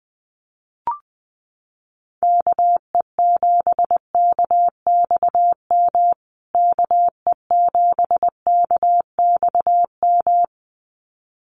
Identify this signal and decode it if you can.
Morse code, heard as a keyed tone, about 20 wpm, 700 Hz. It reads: KE7KXM KE7KXM
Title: USA Call Signs Repeated 2x 20wpm